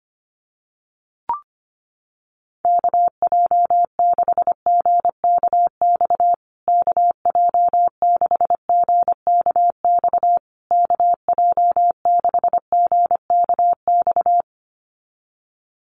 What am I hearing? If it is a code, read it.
Morse code, heard as a keyed tone, about 25 wpm, 700 Hz. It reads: KJ6GKX KJ6GKX KJ6GKX
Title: USA Call Signs Repeated 3x 25wpm